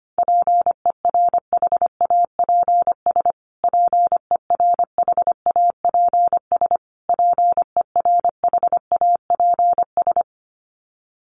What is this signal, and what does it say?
Morse code, heard as a keyed tone, about 25 wpm, 700 Hz. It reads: PERHAPS PERHAPS PERHAPS
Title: Top 500 Words Repeated 25wpm